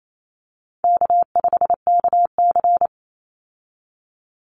Morse code, heard as a keyed tone, about 28 wpm, 700 Hz. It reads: K5KC
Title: Easy Call Signs Repeated 3x - 28wpm